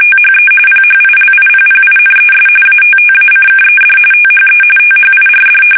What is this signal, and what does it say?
RTTY, about 45.45 baud. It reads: THE PHILIPPINES ATTACKED THE THIRD
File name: fsk850 / battleshipnj